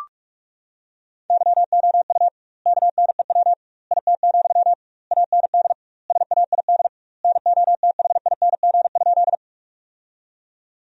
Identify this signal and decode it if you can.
Morse code, heard as a keyed tone, about 46 wpm, 700 Hz. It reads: YOU KNEW IT, AND SAID NOTHING?
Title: 2x Word-Spacing - Sentences from Top 500 Words - 45wpm